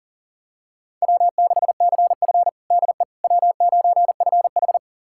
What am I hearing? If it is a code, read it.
Morse code, heard as a keyed tone, about 37 wpm, 700 Hz. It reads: W6CF DE W9FH
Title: USA Callsign DE Callsign - 4 Characters - 40wpm